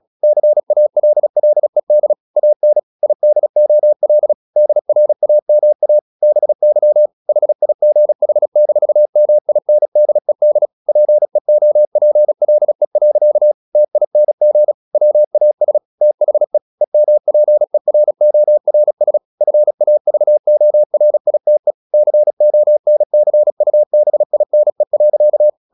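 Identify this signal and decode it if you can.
Morse code, heard as a keyed tone, about 36 wpm, 600 Hz. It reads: CALLED AN IDOL DRAMA BY HIGH-MINDED PEOPLE. TING WAS THE EMPERORS FAVORITE CONCUBINE.
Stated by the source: Morse practice at 36 wpm, text of Ken Follett's novel Never, from CW med Gnister.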